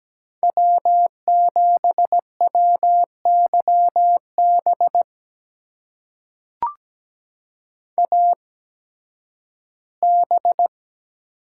Morse code, heard as a keyed tone, about 17 wpm, spaced as for 11 wpm, 700 Hz. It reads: W7WYB AB
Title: USA Call Signs - 5 Characters - 8x Character Spacing - 17wpm